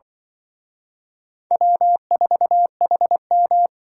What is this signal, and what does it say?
Morse code, heard as a keyed tone, about 24 wpm, 700 Hz. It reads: W4HM
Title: Speed Racing - USA Call Signs 20wpm